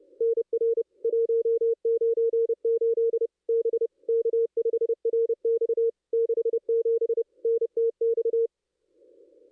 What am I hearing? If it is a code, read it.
Morse code, heard as a keyed tone, about 30 wpm, 460 Hz. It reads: NR 198 B K5RX 67 NTX
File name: contests / ss-cw-k5tr-op-n5rz -0400